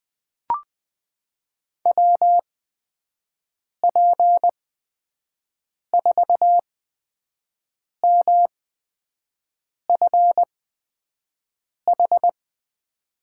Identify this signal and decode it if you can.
Morse code, heard as a keyed tone, about 21 wpm, spaced as for 5 wpm, 700 Hz. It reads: WP4MFH
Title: USA Call Signs - 6 Characters - 8x Character Spacing - 20wpm